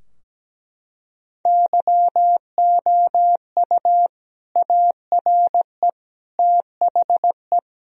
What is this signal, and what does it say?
Morse code, heard as a keyed tone, about 17 wpm, 700 Hz. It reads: YOU ARE THE
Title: Sentences from Top 1000 Words - Review Part 4 - 17wpm